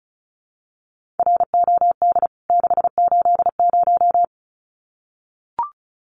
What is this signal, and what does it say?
Morse code, heard as a keyed tone, about 34 wpm, 700 Hz. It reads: ROD 680
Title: CWT Exchange 35wpm